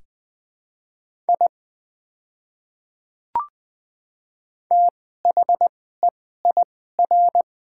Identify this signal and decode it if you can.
Morse code, heard as a keyed tone, about 20 wpm, spaced as for 14 wpm, 700 Hz. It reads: I THEIR
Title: 2x Character Spacing - Top 100 Words - 20wpm